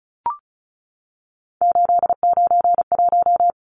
Morse code, heard as a keyed tone, about 34 wpm, 700 Hz. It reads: 891
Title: Numbers - 3-Digit 35wpm